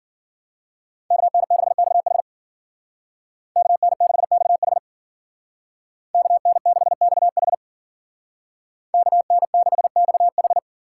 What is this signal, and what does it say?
Morse code, heard as a keyed tone, about 52 wpm, 700 Hz. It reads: KN6XH KN6XH KN6XH KN6XH